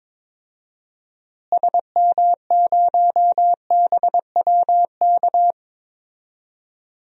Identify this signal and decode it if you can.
Morse code, heard as a keyed tone, about 21 wpm, 700 Hz. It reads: SM0BWK
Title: DXCC-100 Call Signs - 6 Characters - Repeated 3x - 22wpm